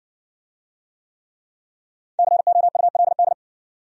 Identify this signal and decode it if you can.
Morse code, heard as a keyed tone, about 56 wpm, 700 Hz. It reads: COULD